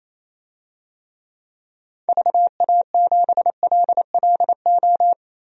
Morse code, heard as a keyed tone, about 28 wpm, 700 Hz. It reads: VA7LLO
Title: DXCC-100 Call Signs - 6 Characters - Repeated 2x - 28wpm